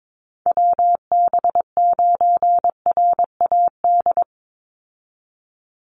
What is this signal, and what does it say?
Morse code, heard as a keyed tone, about 22 wpm, 700 Hz. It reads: WB9RAD
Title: USA Call Signs 22wpm